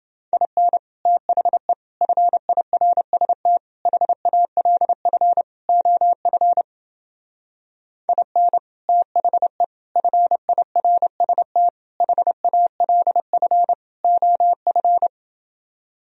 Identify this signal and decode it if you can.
Morse code, heard as a keyed tone, about 29 wpm, 700 Hz. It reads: IN THE FIRST HALF OF IN THE FIRST HALF OF